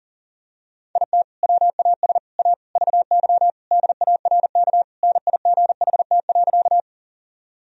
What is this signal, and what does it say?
Morse code, heard as a keyed tone, about 37 wpm, 700 Hz. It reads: IT WAS A VY DARK NIGHT.
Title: Sentences from Top 500 Words and Abbreviations 40wpm